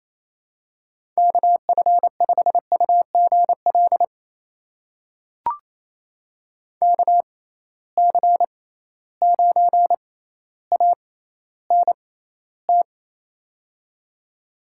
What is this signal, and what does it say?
Morse code, heard as a keyed tone, about 28 wpm, spaced as for 14 wpm, 700 Hz. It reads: KF5UGL KC9ANT
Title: USA Call Signs - 6 Characters - 6x Character Spacing - 28wpm